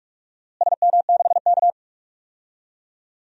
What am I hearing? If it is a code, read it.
Morse code, heard as a keyed tone, about 41 wpm, 700 Hz. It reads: SM6K
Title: DXCC-100 Call Signs - 4 Characters - Repeated 2x - 45wpm